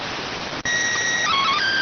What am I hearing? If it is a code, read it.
SSTV (slow-scan television), PD 290.